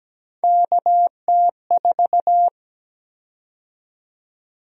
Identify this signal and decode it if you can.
Morse code, heard as a keyed tone, about 17 wpm, 700 Hz. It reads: KT4
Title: USA Call Sign Prefixes - 17wpm